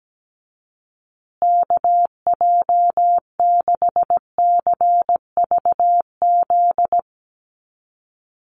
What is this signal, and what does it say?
Morse code, heard as a keyed tone, about 17 wpm, 700 Hz. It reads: KJ6CVZ
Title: USA Call Signs Repeated 2x 17wpm